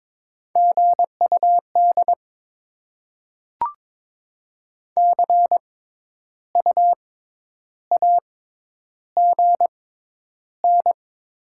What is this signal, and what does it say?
Morse code, heard as a keyed tone, about 21 wpm, spaced as for 10 wpm, 700 Hz. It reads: GUD CU AGN
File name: Top 20 QSO Elements - 6x Character Spacing - 22wpm